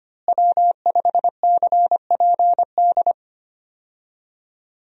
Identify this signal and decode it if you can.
Morse code, heard as a keyed tone, about 25 wpm, 700 Hz. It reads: W5CPD
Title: USA Call Signs - 5 Characters - Repeated 3x - 25wpm